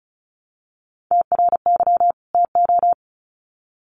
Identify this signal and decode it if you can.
Morse code, heard as a keyed tone, about 34 wpm, 700 Hz. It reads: TRY TO
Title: Sets of 2 Words 35wpm